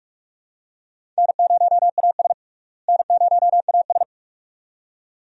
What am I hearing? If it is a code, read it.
Morse code, heard as a keyed tone, about 46 wpm, 700 Hz. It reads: N0AS N0AS
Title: USA Call Signs - 4 Characters - Repeated 2x - 45wpm